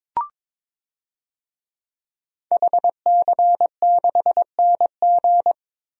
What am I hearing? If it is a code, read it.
Morse code, heard as a keyed tone, about 22 wpm, 700 Hz. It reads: HC6NG
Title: DXCC-100 Call Signs - 5 Characters - 22wpm